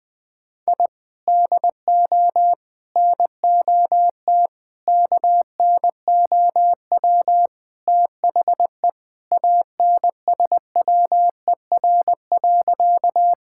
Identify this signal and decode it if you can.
Morse code, heard as a keyed tone, about 21 wpm, 700 Hz. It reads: I DO NOT KNOW THE ANSWER.
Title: Sentences with 5-Word Phrases - 20wpm